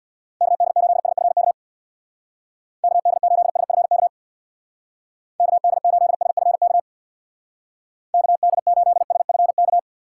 Morse code, heard as a keyed tone, about 60 wpm, 700 Hz. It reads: KD8SFK KD8SFK KD8SFK KD8SFK
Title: Speed Racing - USA 2x3 Call Signs 50wpm